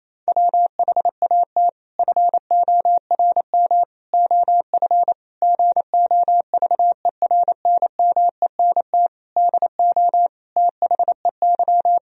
Morse code, heard as a keyed tone, about 28 wpm, 700 Hz. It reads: WHAT FORM OF GOVERNMENT DO THEY
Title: Sentences from Top 1000 Words - Review Part 3 - 28wpm